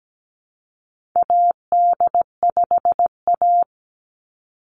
Morse code, heard as a keyed tone, about 17 wpm, 700 Hz. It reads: AD5A
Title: NA SOTA Chaser Callsigns 17wpm